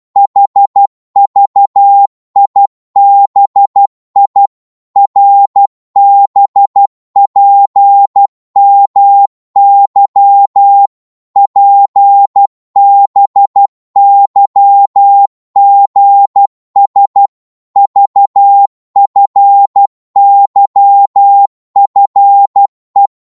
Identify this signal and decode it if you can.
Morse code, heard as a keyed tone, about 12 wpm, 800 Hz. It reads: HVIBI RBPMY PBYGS VFYFE
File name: Morse / ltr-12wpm